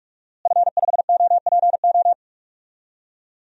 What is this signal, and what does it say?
Morse code, heard as a keyed tone, about 46 wpm, 700 Hz. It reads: F5OPO
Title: DXCC-100 Call Signs - 5 Characters - Repeated 3x - 45wpm